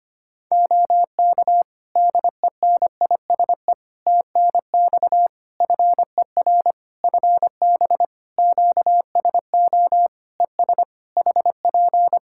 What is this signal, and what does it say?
Morse code, heard as a keyed tone, about 25 wpm, 700 Hz. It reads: OK DENISE TNX FER FB QSO ES HP
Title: K1USN SST QSO Practice 25wpm